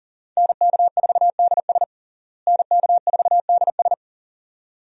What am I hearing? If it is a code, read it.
Morse code, heard as a keyed tone, about 37 wpm, 700 Hz. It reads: NK4DS NK4DS
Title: USA Call Signs - 5 Characters - Repeated 2x - 40wpm